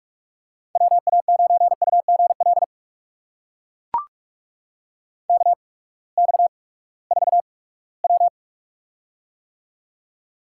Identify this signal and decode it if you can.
Morse code, heard as a keyed tone, about 44 wpm, spaced as for 22 wpm, 700 Hz. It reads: WA9UGL KX4W
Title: USA Call Signs - 8x Character Spacing - 45wpm